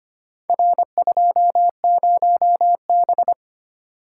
Morse code, heard as a keyed tone, about 25 wpm, 700 Hz. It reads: R20B
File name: practice / 25wpm / Four Letter-Number Combinations - 25wpm